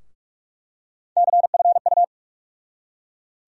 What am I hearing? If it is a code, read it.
Morse code, heard as a keyed tone, about 43 wpm, 700 Hz. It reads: CFU
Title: USA Call Sign Postfixes - 3 Letters 45wpm